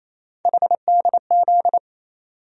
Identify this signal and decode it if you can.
Morse code, heard as a keyed tone, about 28 wpm, 700 Hz. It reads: HDZ
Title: USA Call Sign Postfixes - 3 Letters 28wpm